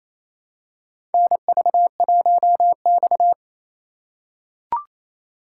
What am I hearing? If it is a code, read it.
Morse code, heard as a keyed tone, about 28 wpm, 700 Hz. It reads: NV1X
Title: USA Call Signs - 4 Characters - 3x Character Spacing - 28wpm